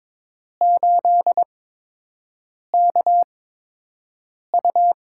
Morse code, heard as a keyed tone, about 22 wpm, spaced as for 6 wpm, 700 Hz. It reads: 8KU
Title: USA Call Signs - 8x Character Spacing - 22wpm